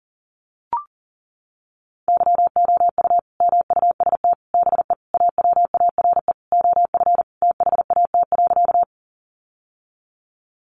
Morse code, heard as a keyed tone, about 40 wpm, 700 Hz. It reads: YOU MUST BE AWARE OF THAT.
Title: Sentences from Top 1200 Words 40wpm